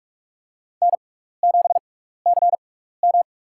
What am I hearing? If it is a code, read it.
Morse code, heard as a keyed tone, about 41 wpm, spaced as for 15 wpm, 700 Hz. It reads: N7CM
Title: USA Call Signs - 4 Characters - 6x Character Spacing - 45wpm